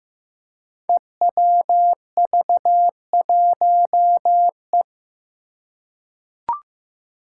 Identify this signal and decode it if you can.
Morse code, heard as a keyed tone, about 15 wpm, 700 Hz. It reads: EWV1E
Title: Five Challenging Letter-Number Combinations 15wpm